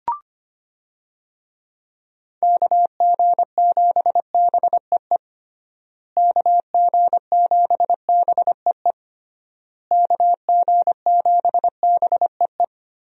Morse code, heard as a keyed tone, about 25 wpm, 700 Hz. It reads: KG7BEE KG7BEE KG7BEE
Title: USA Call Signs - 6 Characters - Repeated 3x - 25wpm